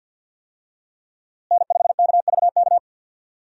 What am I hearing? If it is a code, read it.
Morse code, heard as a keyed tone, about 52 wpm, 700 Hz. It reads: N5KVK
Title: USA Call Signs - 5 Characters - Repeated 3x - 50wpm